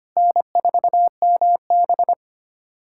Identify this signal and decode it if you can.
Morse code, heard as a keyed tone, about 25 wpm, 700 Hz. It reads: N4MB
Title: USA Callsign DE Callsign - 4 Characters - 25wpm